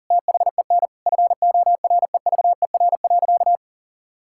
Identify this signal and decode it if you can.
Morse code, heard as a keyed tone, about 40 wpm, 700 Hz. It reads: THEN FOREVER.
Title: Sentences from Top 1600 Words 40wpm